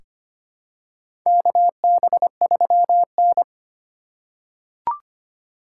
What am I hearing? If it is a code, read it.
Morse code, heard as a keyed tone, about 25 wpm, 700 Hz. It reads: KB3N
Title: USA Call Signs - 4 Characters - 4x Character Spacing - 25wpm